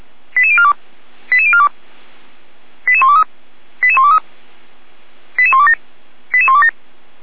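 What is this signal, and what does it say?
Five-tone selective call (ZVEI), heard as quick runs of five tones, tones of about 70 ms. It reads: 88042 88042 88123 88123 88127 88127
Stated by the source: ZVEI five-tone selective call 88129 received with an SDR.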